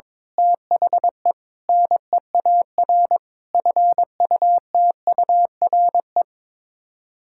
Morse code, heard as a keyed tone, about 22 wpm, 700 Hz. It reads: THE NEAR FUTURE
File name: practice / 22wpm / Sets of 7 Words - 22wpm